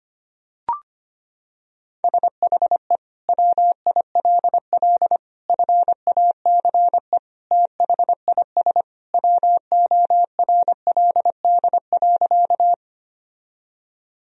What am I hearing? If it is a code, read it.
Morse code, heard as a keyed tone, about 25 wpm, 700 Hz. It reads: SHE WILL FACE THIS WORLD.
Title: Sentences from Top 1000 Words - Review Part 2 - 25wpm